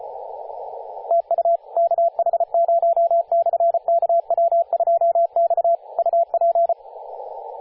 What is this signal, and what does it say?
Morse code, heard as a keyed tone, about 33 wpm, 680 Hz, with radio noise. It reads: TU KH0/KW2X UP
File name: posts / 2016-sep-22-0537z-kh0-kw2x-18-077mhz-cw